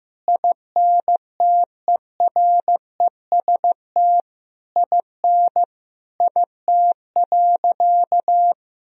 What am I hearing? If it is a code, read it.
Morse code, heard as a keyed tone, about 15 wpm, 700 Hz. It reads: INTEREST IN IT.
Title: Sentences from Top 1000 Words - Review Part 4 - 15wpm